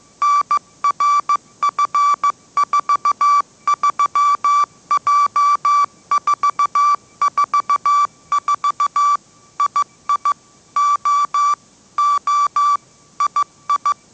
Morse code, heard as a keyed tone, about 15 wpm, 1160 Hz. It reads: NRF43J444 II O O II
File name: mrx / mrx066 16wpm